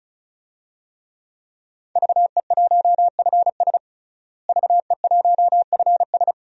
Morse code, heard as a keyed tone, about 34 wpm, 700 Hz. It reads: VE1FS VE1FS
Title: DXCC-100 Call Signs - 5 Characters - Repeated 2x - 35wpm